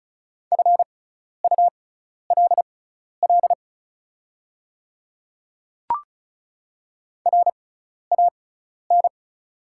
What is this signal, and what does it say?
Morse code, heard as a keyed tone, about 36 wpm, spaced as for 12 wpm, 690 Hz. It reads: FULL RAN
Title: 6x Character Spacing - Top 500 Words - 35wpm